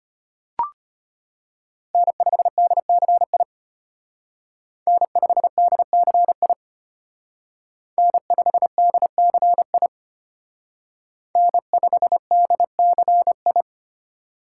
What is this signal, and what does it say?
Morse code, heard as a keyed tone, about 30 wpm, 700 Hz. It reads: N5DCI N5DCI N5DCI N5DCI
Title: Speed Racing - USA 1x3 Call Signs 25wpm